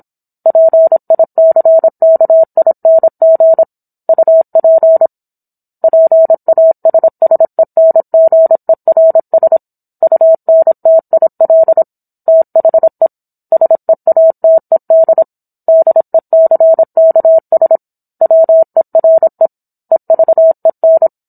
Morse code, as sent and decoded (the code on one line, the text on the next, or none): .--. .. -.-. -.- .. -. --. / ..- .--. / .--. .- ... ... -. --. .-. ... / ..- -. - .. .-.. / - .... / ... .- - . -.. / -.. -.-. -.- ... / .-- . .-. / . ...- . -.
PICKING UP PASSENGERS UNTIL THE SEATED DECKS WERE EVEN